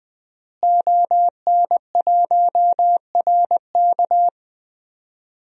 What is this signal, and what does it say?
Morse code, heard as a keyed tone, about 20 wpm, 700 Hz. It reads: ON1RK
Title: DXCC-100 Call Signs - 5 Characters - Repeated 3x - 20wpm